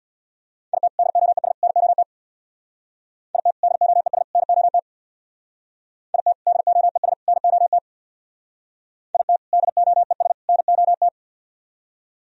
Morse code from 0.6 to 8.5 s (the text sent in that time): IT DOES NOT IT DOES NOT IT DOES NOT